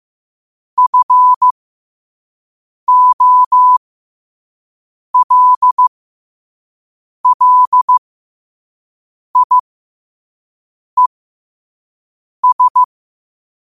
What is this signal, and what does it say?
Morse code, heard as a keyed tone, about 15 wpm, spaced as for 5 wpm, 1000 Hz. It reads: FOLLIES
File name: qotdfiles / 5wpm